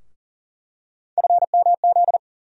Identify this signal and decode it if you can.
Morse code, heard as a keyed tone, about 38 wpm, 700 Hz. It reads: FMZ
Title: USA Call Sign Postfixes - 3 Letters 40wpm